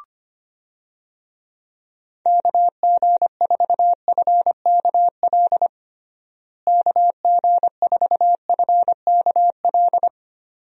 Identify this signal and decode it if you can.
Morse code, heard as a keyed tone, about 25 wpm, 700 Hz. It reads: KG4FKL KG4FKL
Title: USA Call Signs - 6 Characters - Repeated 2x - 25wpm